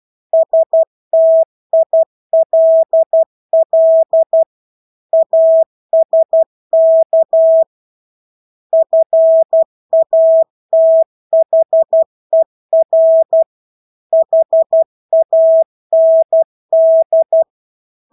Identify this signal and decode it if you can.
Morse code, heard as a keyed tone, about 12 wpm, 650 Hz. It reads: STILL ASK FATHER HAND
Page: Engelske ord 12 wpm